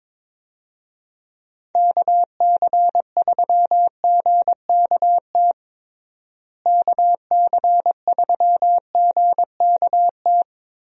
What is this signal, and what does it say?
Morse code, heard as a keyed tone, about 22 wpm, 700 Hz. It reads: KC3GKT KC3GKT
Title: USA Call Signs - 6 Characters - Repeated 3x - 22wpm